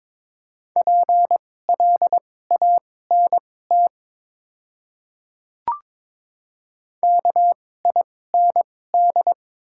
Morse code, heard as a keyed tone, about 22 wpm, spaced as for 16 wpm, 700 Hz. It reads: PLANT KIND
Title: 2x Character Spacing - Top 300 Words - 22wpm